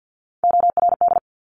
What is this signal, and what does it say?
Morse code, heard as a keyed tone, about 50 wpm, 700 Hz. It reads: ORD
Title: Top Three Letter Combinations in Words 50wpm